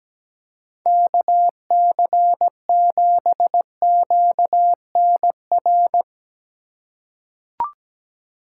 Morse code, heard as a keyed tone, about 17 wpm, 700 Hz. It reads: KC7QNR
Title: USA Call Signs - 6 Characters - 3x Character Spacing - 17wpm